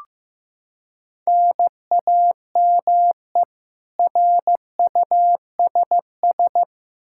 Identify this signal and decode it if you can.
Morse code, heard as a keyed tone, about 15 wpm, 700 Hz. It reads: NAME RUSS
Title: K1USN SST QSO Practice 15wpm